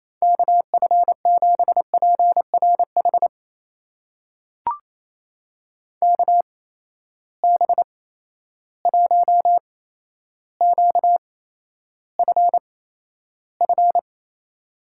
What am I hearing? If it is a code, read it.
Morse code, heard as a keyed tone, about 28 wpm, spaced as for 12 wpm, 700 Hz. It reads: KF7PRH KB1QFF
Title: USA Call Signs - 6 Characters - 8x Character Spacing - 28wpm